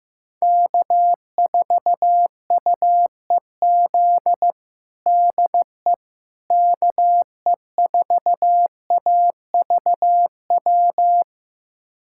Morse code, heard as a keyed tone, about 16 wpm, 700 Hz. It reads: K4UEZ DE KE4AVW
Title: USA Callsign DE Callsign 15wpm